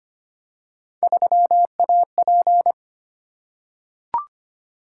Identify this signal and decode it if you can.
Morse code, heard as a keyed tone, about 25 wpm, 700 Hz. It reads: 3AP
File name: Three Letter-Number Combinations - 25wpm